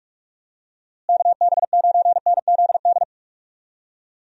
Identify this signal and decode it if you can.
Morse code, heard as a keyed tone, about 45 wpm, 700 Hz. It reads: KB9NZD
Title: USA Call Signs - 6 Characters - Repeated 2x - 45wpm